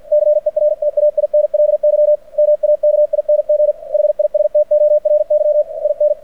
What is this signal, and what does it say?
Morse code, heard as a keyed tone, about 52 wpm, 600 Hz. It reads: CERTAINLY MAKING HISTORY IN
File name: archived / CRSnet